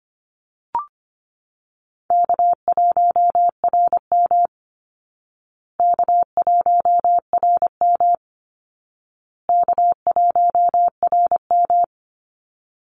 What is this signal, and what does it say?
Morse code, heard as a keyed tone, about 25 wpm, 700 Hz. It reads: K1RM K1RM K1RM
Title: Call Signs Repeated 2x - 25wpm